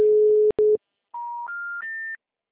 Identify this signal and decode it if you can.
Special information tone.
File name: out- -2033- - - .84308